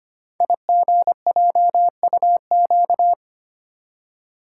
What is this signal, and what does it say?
Morse code, heard as a keyed tone, about 25 wpm, 700 Hz. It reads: IGJUQ
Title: Code Groups - 5 Letters - 25wpm